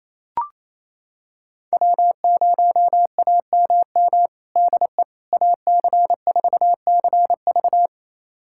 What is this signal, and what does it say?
Morse code, heard as a keyed tone, about 28 wpm, 700 Hz. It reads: W0AMM DE AC4CV
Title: USA Callsign DE Callsign 28wpm